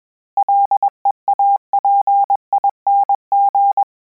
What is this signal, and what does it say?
Morse code, heard as a keyed tone, about 21 wpm, 800 Hz. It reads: LEAPING
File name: Combinations / Combinations-mp3s / Combinations@20